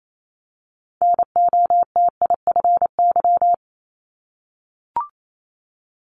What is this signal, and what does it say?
Morse code, heard as a keyed tone, about 28 wpm, 700 Hz. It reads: NOTIFY
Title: Multi-letter Phonemes 28wpm